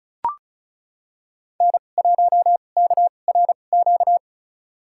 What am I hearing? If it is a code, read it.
Morse code, heard as a keyed tone, about 34 wpm, spaced as for 25 wpm, 700 Hz. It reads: N1KRQ
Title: USA Call Signs - 2x Character Spacing - 35wpm